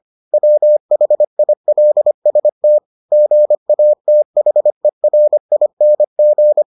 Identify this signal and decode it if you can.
Morse code, heard as a keyed tone, about 25 wpm, 600 Hz. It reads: WHILST GATHERING